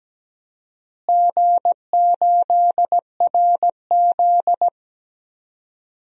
Morse code, heard as a keyed tone, about 17 wpm, 700 Hz. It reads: G8RZ
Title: DXCC-100 Call Signs - 4 Characters - Repeated 3x - 17wpm